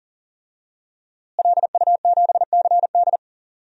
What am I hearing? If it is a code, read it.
Morse code, heard as a keyed tone, about 40 wpm, 700 Hz. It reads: LU7CD
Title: DXCC-100 Call Signs - 5 Characters - Repeated 3x - 40wpm